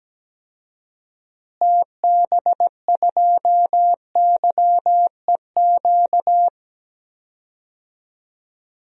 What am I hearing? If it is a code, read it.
Morse code, heard as a keyed tone, about 17 wpm, 700 Hz. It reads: TB2YEQ
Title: DXCC-100 Call Signs - 6 Characters - Repeated 2x - 17wpm